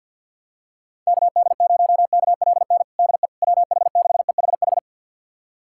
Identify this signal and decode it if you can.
Morse code, heard as a keyed tone, about 51 wpm, 700 Hz. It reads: KD0KLN DE WH6EHH
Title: USA Callsign DE Callsign - 6 Characters - 50wpm